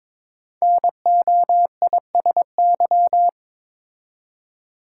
Morse code, heard as a keyed tone, about 22 wpm, 700 Hz. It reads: NOISY
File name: 5 Letter Words - First 3 Letters Given - 22wpm